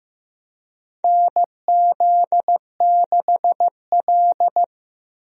Morse code, heard as a keyed tone, about 15 wpm, 700 Hz. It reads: NZ6L
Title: USA Call Signs - 4 Characters - Repeated 2x - 15wpm